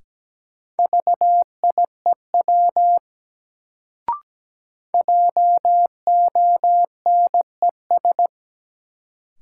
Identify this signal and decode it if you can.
Morse code, heard as a keyed tone, about 17 wpm, 700 Hz. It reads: VIEW JONES
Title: Sets of 5 Words - New Word Drill - 17wpm